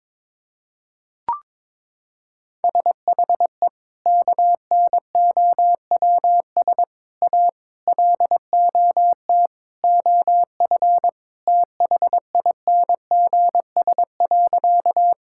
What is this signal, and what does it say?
Morse code, heard as a keyed tone, about 22 wpm, 700 Hz. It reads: SHE KNOWS A LOT OF THINGS.
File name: practice / 22wpm / Sentences with 4-Word Phrases - 22wpm